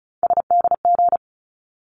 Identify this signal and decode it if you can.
Morse code, heard as a keyed tone, about 34 wpm, 700 Hz. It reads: SDG
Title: ARRL Sections 35wpm